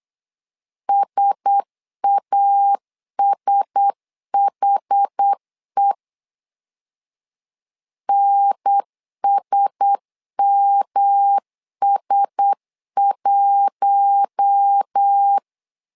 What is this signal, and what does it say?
Morse code, heard as a keyed tone, about 8 wpm, 800 Hz. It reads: SASHE NSMS1
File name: LEZIONI / 4 Q1S 8wpm 2710